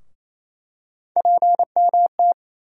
Morse code, heard as a keyed tone, about 28 wpm, 700 Hz. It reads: PMT
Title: USA Call Sign Postfixes - 3 Letters 28wpm